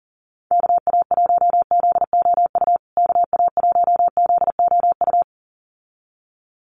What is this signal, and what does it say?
Morse code, heard as a keyed tone, about 40 wpm, 700 Hz. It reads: KA1ZOU KA1ZOU
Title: USA Call Signs Repeated 2x 40wpm